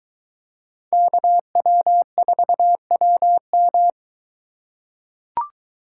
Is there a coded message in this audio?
Yes: Morse code from 0.9 to 3.9 s, heard as a keyed tone, about 22 wpm, 700 Hz. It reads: KW4WM